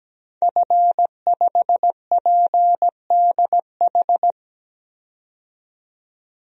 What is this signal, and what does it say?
Morse code, heard as a keyed tone, about 17 wpm, 700 Hz. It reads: F5PDH
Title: DXCC-100 Call Signs - 5 Characters - 17wpm